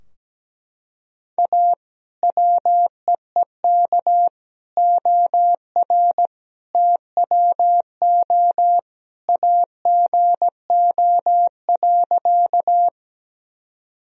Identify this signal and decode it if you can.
Morse code, heard as a keyed tone, about 17 wpm, 700 Hz. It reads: A WEEK OR TWO AGO.
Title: Sentences from Top 1000 Words - Review Part 3 - 17wpm